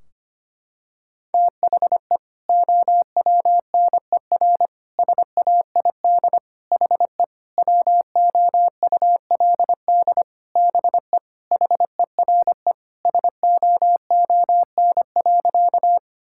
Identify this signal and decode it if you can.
Morse code, heard as a keyed tone, about 25 wpm, 700 Hz. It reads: THE OWNER SAID HE WOULD BE HERE SOON.
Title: Sentences from Top 1600 Words 25wpm